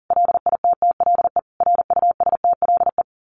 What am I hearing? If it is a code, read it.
Morse code, heard as a keyed tone, about 40 wpm, 700 Hz. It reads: LITTLE RUSTLE
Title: A Sound Of Thunder 40wpm